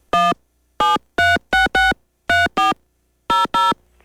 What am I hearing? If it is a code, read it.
DTMF, heard as telephone touch tones, tones of about 170 ms. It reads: 1 736634 00